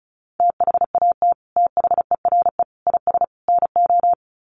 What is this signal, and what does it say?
Morse code, heard as a keyed tone, about 34 wpm, 700 Hz. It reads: THAT THERE IS NO
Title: Sets of 4 Words 35wpm